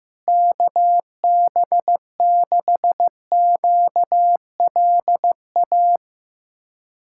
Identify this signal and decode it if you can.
Morse code, heard as a keyed tone, about 15 wpm, 700 Hz. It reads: KB6QLA
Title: USA Callsign DE Callsign 15wpm